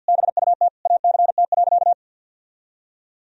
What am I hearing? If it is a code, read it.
Morse code, heard as a keyed tone, about 52 wpm, 700 Hz. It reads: BUT ACT.
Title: Sentences from Top 1000 Words - Review Part 1 - 50wpm